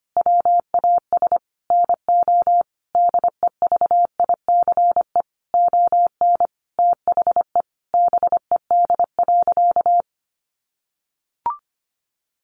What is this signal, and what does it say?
Morse code, heard as a keyed tone, about 25 wpm, 700 Hz. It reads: WAS NO DEVICE ON THE BED.
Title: Sentences from Top 1500 Words 25wpm